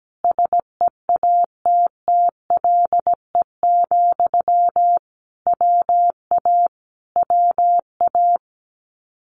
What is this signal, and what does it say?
Morse code, heard as a keyed tone, about 17 wpm, 700 Hz. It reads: SEATTLE, WA WA
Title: Common QSO Phrases Encore 17wpm